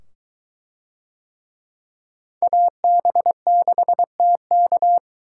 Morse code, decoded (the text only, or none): AB6TK